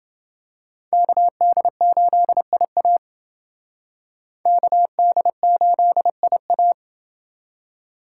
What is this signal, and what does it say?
Morse code, heard as a keyed tone, about 28 wpm, 700 Hz. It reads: KD8IA KD8IA